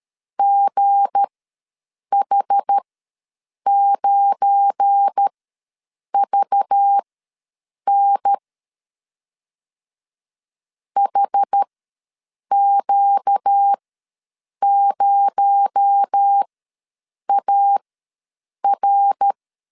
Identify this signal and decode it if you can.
Morse code, heard as a keyed tone, about 13 wpm, spaced as for 7 wpm, 800 Hz. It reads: GH9VN HQ0AR
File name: LEZIONI / 9 286 ws 12wpm 3351